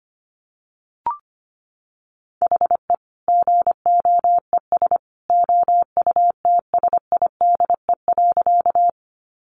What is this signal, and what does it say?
Morse code, heard as a keyed tone, about 25 wpm, 700 Hz. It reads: HE GOES OUTSIDE.
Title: Additional Easy Phrases 25wpm